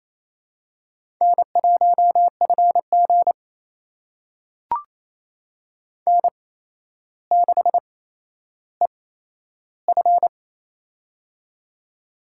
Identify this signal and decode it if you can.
Morse code, heard as a keyed tone, about 28 wpm, spaced as for 12 wpm, 700 Hz. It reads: N1FG N6EF
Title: USA Call Signs - 4 Characters - 8x Character Spacing - 28wpm